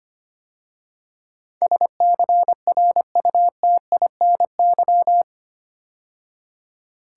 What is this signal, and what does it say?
Morse code, heard as a keyed tone, about 25 wpm, 700 Hz. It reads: SCRUTINY
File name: 8 Letter Words - First 3 Letters Given - 25wpm